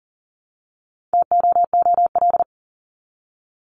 Morse code, heard as a keyed tone, about 40 wpm, 700 Hz. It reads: TOOL
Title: Top 751-1000 Words 40wpm